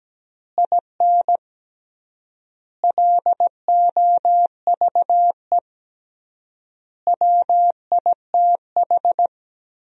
Morse code, heard as a keyed tone, about 17 wpm, 700 Hz. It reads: IN LOVE WITH
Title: 3x Word-Spacing - Sentences from Top 400 Words - 17wpm